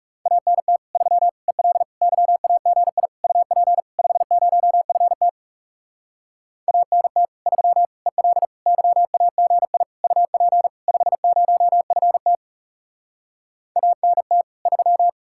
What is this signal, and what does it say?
Morse code, heard as a keyed tone, about 40 wpm, 700 Hz. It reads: ANT 3 EL YAGI UP 50FT ANT 3 EL YAGI UP 50FT ANT 3